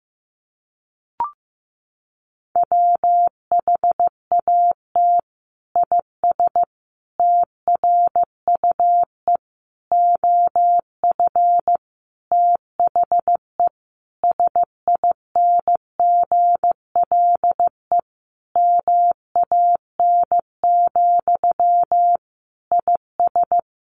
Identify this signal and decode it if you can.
Morse code, heard as a keyed tone, about 15 wpm, 700 Hz. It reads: WHAT IS TRUE OF THE SINGLE MAN, IS
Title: Sentences from Top 1400 Words 15wpm